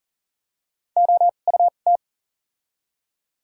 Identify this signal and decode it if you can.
Morse code, heard as a keyed tone, about 40 wpm, spaced as for 29 wpm, 700 Hz. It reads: OUT
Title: 2x Character Spacing - Top 100 Words - 40wpm